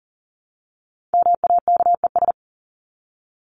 Morse code, heard as a keyed tone, about 40 wpm, 700 Hz. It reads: MAKES
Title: ES:S Suffix - Top 100 Words with Base Words 40wpm